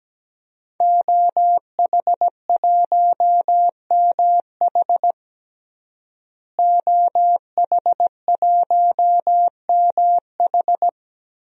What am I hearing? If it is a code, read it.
Morse code, heard as a keyed tone, about 17 wpm, 700 Hz. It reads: OH1MH OH1MH
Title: DXCC-100 Call Signs - 5 Characters - Repeated 2x - 17wpm